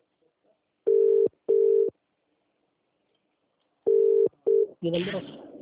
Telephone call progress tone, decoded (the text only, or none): ringback tone